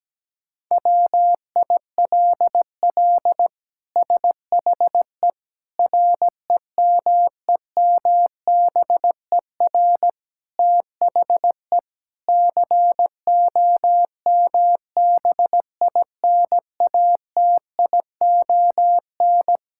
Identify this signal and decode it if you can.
Morse code, heard as a keyed tone, about 17 wpm, 700 Hz. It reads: WILL SHE REMEMBER THE COMBINATION
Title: Sentences from Top 1000 Words - Review Part 6 - 17wpm